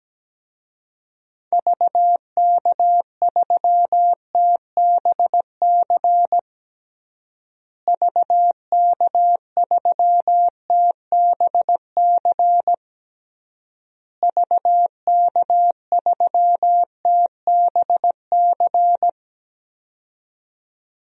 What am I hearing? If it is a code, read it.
Morse code, heard as a keyed tone, about 17 wpm, 700 Hz. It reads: VK3TBC VK3TBC VK3TBC
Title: DXCC-100 Call Signs - 6 Characters - Repeated 3x - 17wpm